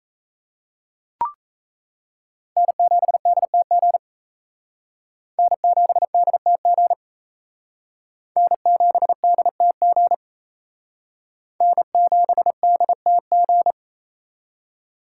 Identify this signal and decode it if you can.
Morse code, heard as a keyed tone, about 36 wpm, 700 Hz. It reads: N7DTG N7DTG N7DTG N7DTG